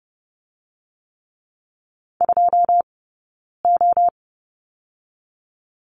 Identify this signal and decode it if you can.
Morse code, heard as a keyed tone, about 30 wpm, 700 Hz. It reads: 2 O
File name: Lesson 19d - Letter w - Two Characters 30wpm